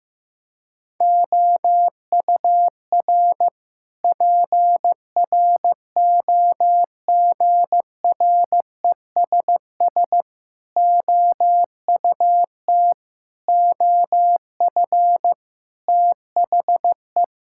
Morse code, heard as a keyed tone, about 15 wpm, 700 Hz. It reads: OUR PROGRESS OUT OF THE
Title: Sentences from Top 1600 Words 15wpm